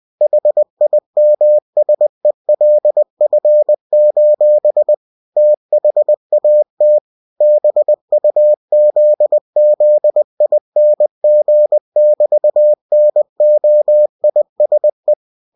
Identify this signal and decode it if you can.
Morse code, heard as a keyed tone, about 20 wpm, 600 Hz. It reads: HIMSELF: THAT BUZZING<BT>NOISE
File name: cw-books / winnie-the-pooh / winnie-20wpm-page-0002